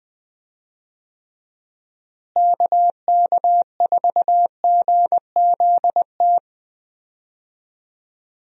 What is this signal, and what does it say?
Morse code, heard as a keyed tone, about 20 wpm, 700 Hz. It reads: KK4GZT
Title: USA Call Signs - 6 Characters - Repeated 3x - 20wpm